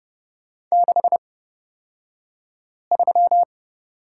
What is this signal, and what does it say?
Morse code, heard as a keyed tone, about 30 wpm, spaced as for 5 wpm, 700 Hz. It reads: 63
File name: Lesson 37d - Letter x - Two Characters 30wpm